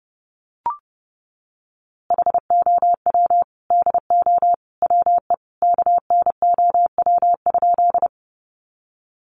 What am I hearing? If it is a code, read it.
Morse code, heard as a keyed tone, about 28 wpm, 700 Hz. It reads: HOW DO WE KNOW?
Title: A Sound Of Thunder 30wpm